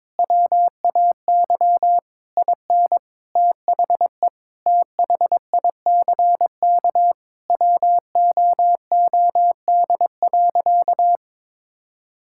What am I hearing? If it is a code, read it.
Morse code, heard as a keyed tone, about 22 wpm, 700 Hz. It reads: WAY IN THE THICK WOOD.
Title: Sentences from Top 1600 Words 22wpm